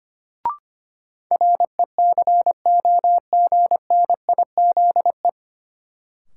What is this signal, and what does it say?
Morse code, heard as a keyed tone, about 25 wpm, 700 Hz. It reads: RECOGNIZE
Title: Sentences from Top 1400 Words - New Word Drill Alt - 25wpm